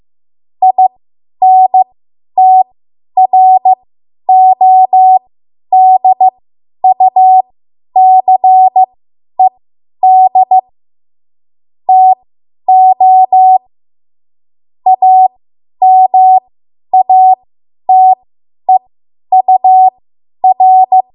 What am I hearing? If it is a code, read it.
Morse code, heard as a keyed tone, about 15 wpm, spaced as for 10 wpm, 750 Hz. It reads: INTRODUCED TO AMATEUR
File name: Morse / 10WPM